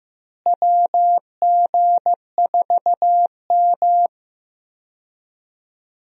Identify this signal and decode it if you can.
Morse code, heard as a keyed tone, about 16 wpm, 700 Hz. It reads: WG4M
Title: USA Call Signs - 4 Characters - Repeated 3x - 15wpm